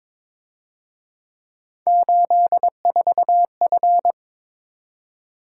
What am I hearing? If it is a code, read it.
Morse code, heard as a keyed tone, about 26 wpm, 700 Hz. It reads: 84F
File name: Speed-Racing - Three Letter-Number Combinations - 22wpm